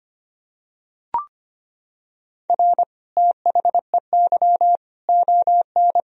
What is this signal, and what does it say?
Morse code, heard as a keyed tone, about 25 wpm, 700 Hz. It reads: R THEY ON
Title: Sentences from Top 500 Words and Abbreviations 25wpm